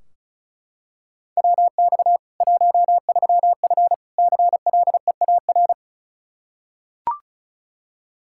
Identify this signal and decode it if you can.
Morse code, heard as a keyed tone, about 36 wpm, 700 Hz. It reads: WX 13F CLEAR
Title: K1USN SST QSO Practice 35wpm